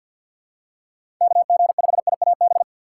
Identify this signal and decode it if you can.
Morse code, heard as a keyed tone, about 51 wpm, 700 Hz. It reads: KG5IAB
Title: USA Call Signs - 6 Characters - 4x Character Spacing - 50wpm